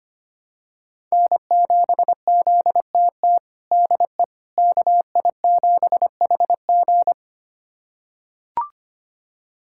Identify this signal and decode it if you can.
Morse code, heard as a keyed tone, about 25 wpm, 700 Hz. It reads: N7ZTT DE KI7HG
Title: USA Callsign DE Callsign - 5 Characters - 25wpm